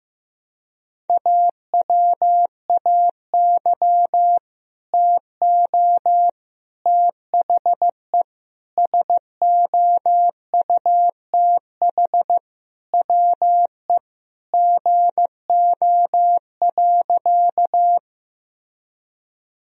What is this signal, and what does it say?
Morse code, heard as a keyed tone, about 15 wpm, 700 Hz. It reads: AWAY TO THE SOUTH WE GO.
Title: Sentences from Top 1000 Words - Review Part 2 - 15wpm